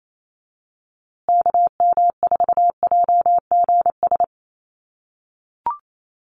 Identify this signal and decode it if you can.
Morse code, heard as a keyed tone, about 28 wpm, 700 Hz. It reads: KM4JGS
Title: USA Call Signs Repeated 3x 28wpm